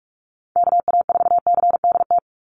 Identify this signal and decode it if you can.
Morse code, heard as a keyed tone, about 44 wpm, 700 Hz. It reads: KA4CDT
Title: USA Call Signs 45wpm